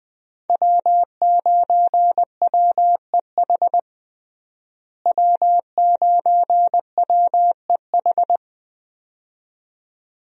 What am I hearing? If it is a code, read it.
Morse code, heard as a keyed tone, about 20 wpm, 700 Hz. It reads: W9WEH W9WEH
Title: USA Call Signs - 5 Characters - Repeated 2x - 20wpm